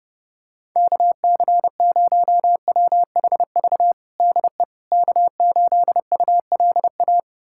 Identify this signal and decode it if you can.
Morse code, heard as a keyed tone, about 32 wpm, 700 Hz. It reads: KC0WHV DE K8ULA
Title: USA Callsign DE Callsign 30wpm